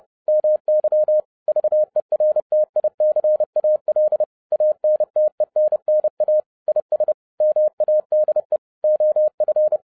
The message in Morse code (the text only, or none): MY VERTICAL ANTENNA IS MADE OF